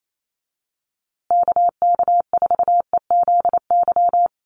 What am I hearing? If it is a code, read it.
Morse code, heard as a keyed tone, about 28 wpm, 700 Hz. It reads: KK4EZY
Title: USA Call Signs Repeated 2x 28wpm